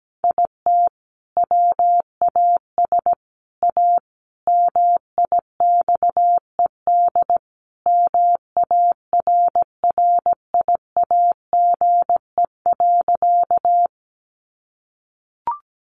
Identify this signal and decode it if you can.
Morse code, heard as a keyed tone, about 17 wpm, 700 Hz. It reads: IT WAS A MIXED MARRIAGE.
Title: Sentences from Top 1500 Words 17wpm